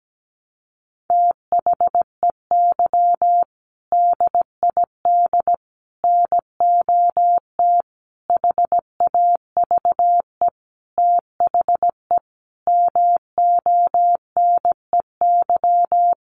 Morse code, heard as a keyed tone, about 17 wpm, 700 Hz. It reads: THEY DID NOT HAVE THE MONEY
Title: Sentences from Top 1500 Words 17wpm